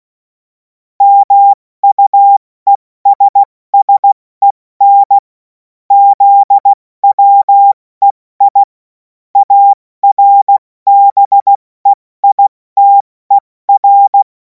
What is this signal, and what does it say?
Morse code, heard as a keyed tone, about 16 wpm, 800 Hz. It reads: MUESSEN ZWEI ARBEITER
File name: Morsekurs / Witze / Witz 17. Februar 2026 70 F80